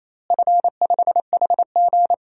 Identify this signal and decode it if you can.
Morse code, heard as a keyed tone, about 28 wpm, 700 Hz. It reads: F5HG
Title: DXCC-100 Call Signs - 4 Characters - Repeated 3x - 28wpm